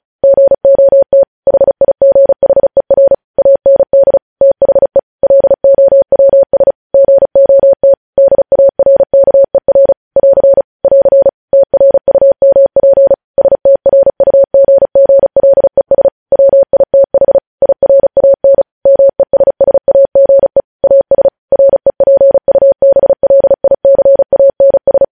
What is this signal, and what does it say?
Morse code, heard as a keyed tone, about 36 wpm, 550 Hz. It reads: GOT HIGHER AND THE LOWS GOT DARKER <AR> <AR> TRUMP STRUGGLES WITH IRAN MESSAGE AS REPUBLICANS
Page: news headlines in morse code at 35 wpm